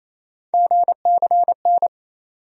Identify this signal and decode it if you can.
Morse code, heard as a keyed tone, about 28 wpm, 700 Hz. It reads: GCN
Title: USA Call Sign Postfixes - 3 Letters 28wpm